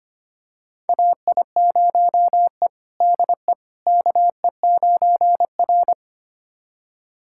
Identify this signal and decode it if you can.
Morse code, heard as a keyed tone, about 25 wpm, 700 Hz. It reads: AI0E DE KE9R
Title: USA Callsign DE Callsign - 4 Characters - 25wpm